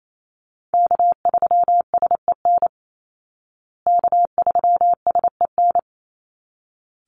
Morse code, heard as a keyed tone, about 28 wpm, 700 Hz. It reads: K3SEN K3SEN
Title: Call Signs Repeated 3x - 28wpm